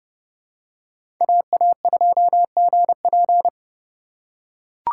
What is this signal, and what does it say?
Morse code, heard as a keyed tone, about 31 wpm, 700 Hz. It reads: AA2GP
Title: USA Call Signs - 4x Character Spacing - 30wpm